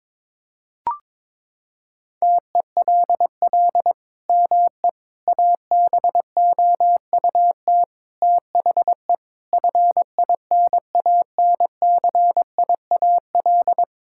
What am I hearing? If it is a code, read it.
Morse code, heard as a keyed tone, about 22 wpm, 700 Hz. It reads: TELL ME ABOUT THE FINANCIAL
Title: Sentences from Top 1000 Words - Review Part 5 - 22wpm